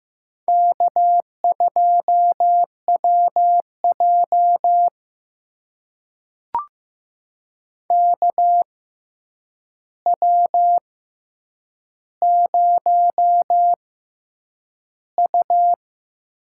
Morse code, heard as a keyed tone, about 15 wpm, spaced as for 8 wpm, 700 Hz. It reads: K2WJ KW0U
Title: USA Call Signs - 4 Characters - 6x Character Spacing - 15wpm